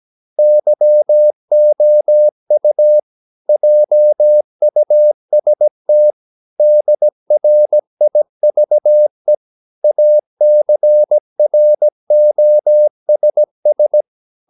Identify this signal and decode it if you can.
Morse code, heard as a keyed tone, about 17 wpm, 600 Hz. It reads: YOU JUST DRIVE ACROSS
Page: Romanen Never af Ken Follet på engelsk.